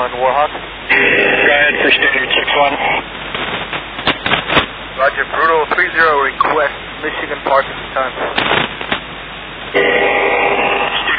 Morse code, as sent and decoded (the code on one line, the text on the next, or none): none